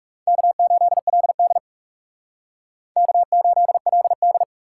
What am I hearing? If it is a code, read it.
Morse code, heard as a keyed tone, about 37 wpm, 700 Hz. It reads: K8LD K8LD
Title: Speed Racing - USA 1x2 Call Signs 30wpm